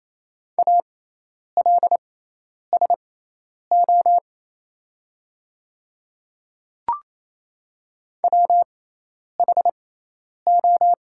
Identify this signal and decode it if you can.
Morse code, heard as a keyed tone, about 27 wpm, spaced as for 10 wpm, 700 Hz. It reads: ALSO WHO
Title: 6x Character Spacing - Top 200 Words - 28wpm